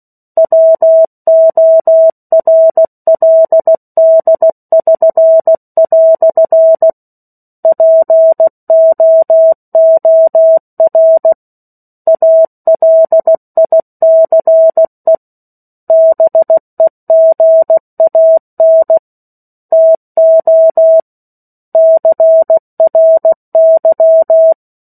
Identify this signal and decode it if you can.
Morse code, heard as a keyed tone, about 16 wpm, 650 Hz. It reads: WORLD<SN>" POOR ALICE BEGAN TO CRY
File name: 03-Chapter-03 16 WPM